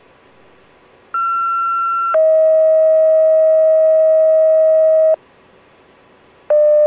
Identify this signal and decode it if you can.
Two-tone page, heard as a one-second tone, then a three-second tone, about 1360 then 640 Hz. It is heard from 1.1 to 5.1 s.